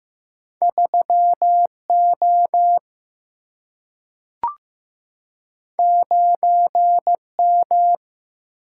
Morse code, heard as a keyed tone, about 15 wpm, 700 Hz. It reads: 3 O 9 M
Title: Single-Letter Single-Number 15wpm